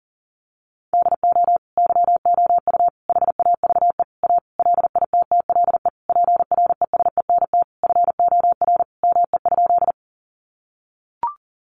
Morse code, heard as a keyed tone, about 37 wpm, 700 Hz. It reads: DO YOU HAVE A LITTLE PRESENT FOR ME?
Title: Sentences from Top 800 Words 40wpm